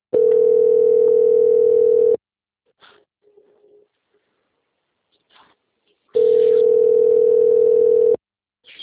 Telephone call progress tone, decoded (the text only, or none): ringback tone